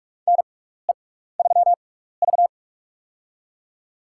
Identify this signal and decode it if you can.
Morse code, heard as a keyed tone, about 43 wpm, spaced as for 15 wpm, 700 Hz. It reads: NE3V
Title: USA Call Signs - 4 Characters - 6x Character Spacing - 45wpm